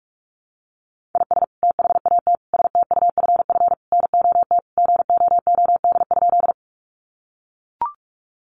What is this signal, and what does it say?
Morse code, heard as a keyed tone, about 44 wpm, 700 Hz. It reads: IS THAT STUFF NOT GOOD?
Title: Sentences from Top 1400 Words 45wpm